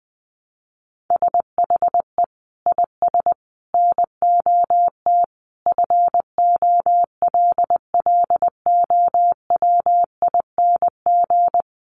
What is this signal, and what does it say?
Morse code, heard as a keyed tone, about 20 wpm, 700 Hz. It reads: SHE IS NOT FOLLOWING
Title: Sentences from Top 700 Words 20wpm